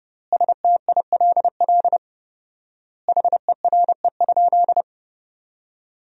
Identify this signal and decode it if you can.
Morse code, heard as a keyed tone, about 30 wpm, 700 Hz. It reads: STILL HERE?
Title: 4x Word-Spacing - Sentences from Top 300 Words - 30wpm